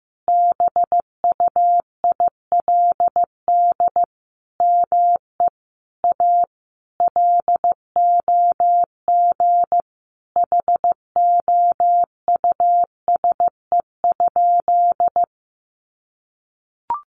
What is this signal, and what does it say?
Morse code, heard as a keyed tone, about 14 wpm, 700 Hz. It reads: BUILD ME A LOG HOUSE?
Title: Sentences from Top 1400 Words 15wpm